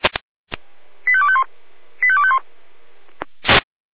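Five-tone selective call (ZVEI), heard as quick runs of five tones, tones of about 70 ms. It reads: 85251 85251
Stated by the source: ZVEI five-tone selective call 85251 received on SDR.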